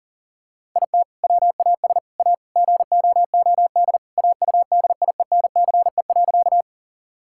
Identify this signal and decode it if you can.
Morse code, heard as a keyed tone, about 40 wpm, 700 Hz. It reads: IT WAS A GOOD AUDIENCE.
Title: Sentences from Top 1000 Words - Review Part 5 - 40wpm